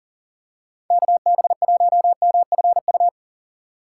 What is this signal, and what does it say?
Morse code, heard as a keyed tone, about 40 wpm, 700 Hz. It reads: KB1MFU
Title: USA Call Signs - 3x Character Spacing - 40wpm